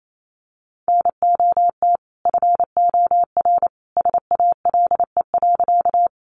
Morse code, heard as a keyed tone, about 28 wpm, 700 Hz. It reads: NOT FOR SALE.
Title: Sentences from Top 1400 Words 28wpm